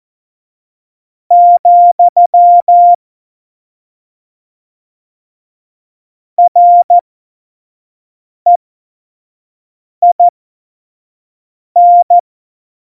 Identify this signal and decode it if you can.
Morse code, heard as a keyed tone, about 14 wpm, spaced as for 5 wpm, 700 Hz. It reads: , REIN
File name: Morsekurs / Witze / Witz 29. Oktober 2019, 25 F70